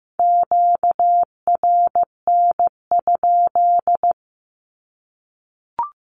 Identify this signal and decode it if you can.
Morse code, heard as a keyed tone, about 15 wpm, 700 Hz. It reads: QRN?
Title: Q-Codes 15wpm